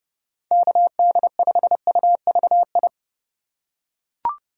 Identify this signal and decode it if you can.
Morse code, heard as a keyed tone, about 31 wpm, 700 Hz. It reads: KD5UVI
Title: USA Call Signs - 6 Characters - 3x Character Spacing - 30wpm